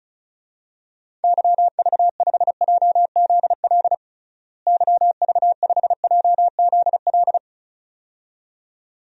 Morse code, heard as a keyed tone, about 34 wpm, 700 Hz. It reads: YV5JZL YV5JZL
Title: DXCC-100 Call Signs - 6 Characters - Repeated 2x - 35wpm